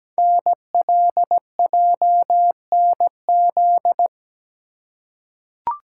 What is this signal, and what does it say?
Morse code, heard as a keyed tone, about 17 wpm, 700 Hz. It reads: NLJNZ
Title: Code Groups - 5 Letters - 17wpm